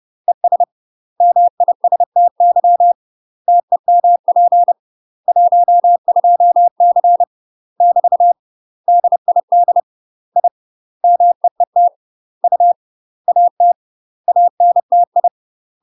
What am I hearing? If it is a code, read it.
Morse code, heard as a keyed tone, about 30 wpm, 700 Hz. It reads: ES MISTY TEMP 12C <BT> DID I MEET U AT ANTI
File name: UK MASTER QSOs TRAINING / 30WPM / UK MASTER QSOs 0043